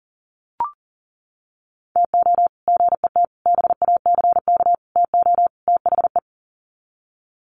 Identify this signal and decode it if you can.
Morse code, heard as a keyed tone, about 40 wpm, 700 Hz. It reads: TO GET BACK TO THE
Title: Sets of 5 Words Encore 40wpm